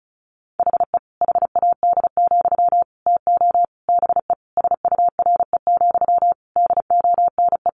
Morse code, heard as a keyed tone, about 34 wpm, 700 Hz. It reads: HE HAD, TO BE SURE, DONE
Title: Sentences from Top 500 Words 35wpm